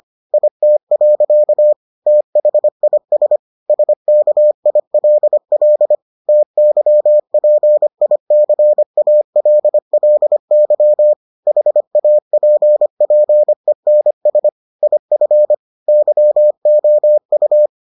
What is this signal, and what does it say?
Morse code, heard as a keyed tone, about 25 wpm, 600 Hz. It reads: IT. THIS SKILL TYPICALLY HAPPENS IF YOU